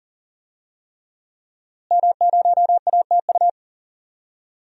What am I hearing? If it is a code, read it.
Morse code, heard as a keyed tone, about 40 wpm, 700 Hz. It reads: M0ATU
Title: DXCC-100 Call Signs - 5 Characters - Repeated 3x - 40wpm